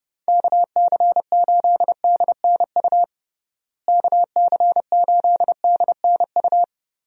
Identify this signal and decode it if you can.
Morse code, heard as a keyed tone, about 30 wpm, 700 Hz. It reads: KC8DNU KC8DNU
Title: USA Call Signs - 6 Characters - Repeated 3x - 30wpm